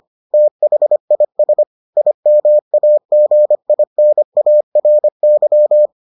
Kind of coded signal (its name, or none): Morse code